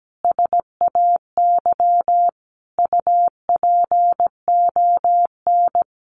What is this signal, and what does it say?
Morse code, heard as a keyed tone, about 17 wpm, 700 Hz. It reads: SAY UPON
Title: Sentences from Top 1200 Words 17wpm